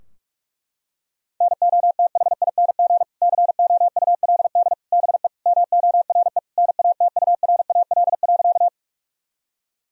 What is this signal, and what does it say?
Morse code, heard as a keyed tone, about 41 wpm, 700 Hz. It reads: NOTHING COULD BE MORE NATURAL.
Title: Sentences from Top 1000 Words - Review Part 4 - 45wpm